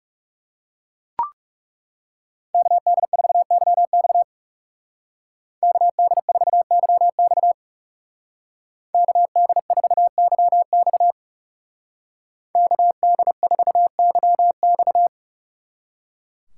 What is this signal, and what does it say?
Morse code, heard as a keyed tone, about 37 wpm, 700 Hz. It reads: KD4YX KD4YX KD4YX KD4YX
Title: Speed Racing - USA 2x2 Call Signs 30wpm